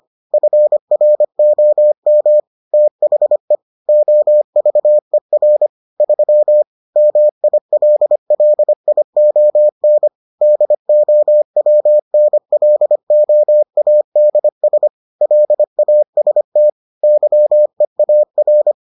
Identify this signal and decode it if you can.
Morse code, heard as a keyed tone, about 25 wpm, 600 Hz. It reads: FROM THE OVER 3 MILLION DOWNLOADS LAST YEAR